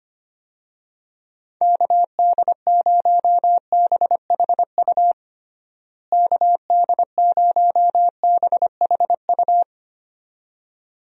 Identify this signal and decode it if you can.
Morse code, heard as a keyed tone, about 25 wpm, 700 Hz. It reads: KD0BHU KD0BHU
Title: USA Call Signs - 6 Characters - Repeated 2x - 25wpm